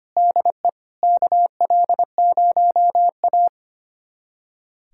Morse code, heard as a keyed tone, about 25 wpm, 700 Hz. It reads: DE KL0A
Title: USA Callsign DE Callsign - 4 Characters - 25wpm